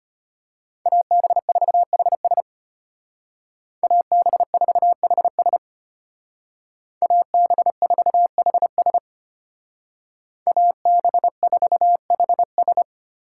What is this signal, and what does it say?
Morse code, heard as a keyed tone, about 30 wpm, 700 Hz. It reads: AB4HS AB4HS AB4HS AB4HS
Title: Speed Racing - USA 2x2 Call Signs 25wpm